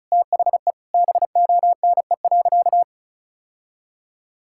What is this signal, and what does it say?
Morse code, heard as a keyed tone, about 34 wpm, 700 Hz. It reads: THE BONE.
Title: Sentences from Top 1600 Words 35wpm